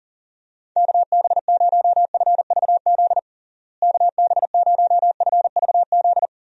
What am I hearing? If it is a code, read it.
Morse code, heard as a keyed tone, about 40 wpm, 700 Hz. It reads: KB0FVZ KB0FVZ
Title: USA Call Signs - 6 Characters - Repeated 3x - 40wpm